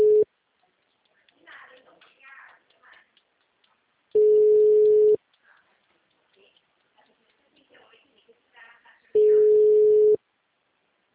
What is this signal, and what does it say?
Ringback tone (Europe, 425 Hz), with speech.